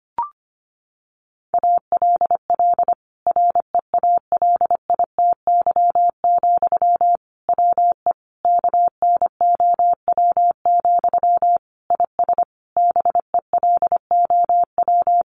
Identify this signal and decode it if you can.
Morse code, heard as a keyed tone, about 25 wpm, 700 Hz. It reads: ALL REALITY, WE KNOW, IS BELOW
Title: Sentences from Top 1400 Words 25wpm